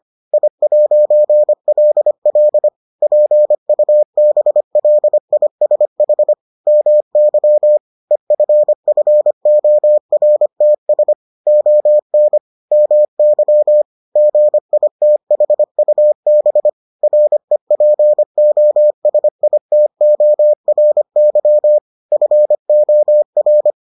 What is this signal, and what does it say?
Morse code, heard as a keyed tone, about 25 wpm, 600 Hz. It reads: I'LL PUBLISH MY EFFORTS ON MY GITHUB REPOSITORY FOR